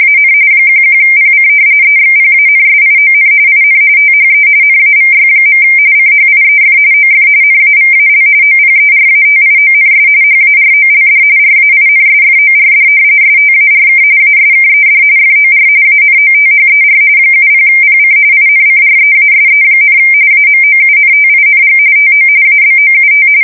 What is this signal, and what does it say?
RTTY, about 45.45 baud. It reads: AN 11PC EXPANSION IN THE NAVY. CHIEF OF NAVAL OPERATIONS HAROLD RAINSFORD STARK ASKED FOR ANOTHER 70PC INCREASE, AMOUNTING TO ABOUT
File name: fsk170 / usnavy4